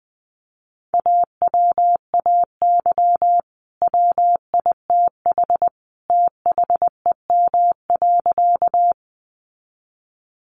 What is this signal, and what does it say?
Morse code, heard as a keyed tone, about 19 wpm, 700 Hz. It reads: AWAY WITH THEM.
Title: Sentences from Top 600 Words 20wpm